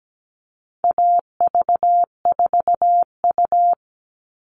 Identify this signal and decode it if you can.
Morse code, heard as a keyed tone, about 17 wpm, 700 Hz. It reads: AV4U
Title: All Cut Numbers - 4-Digit 17wpm